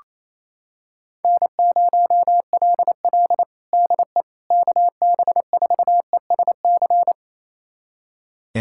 Morse code, heard as a keyed tone, about 28 wpm, 700 Hz. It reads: N0LL DE KB4ESC
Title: USA Callsign DE Callsign 28wpm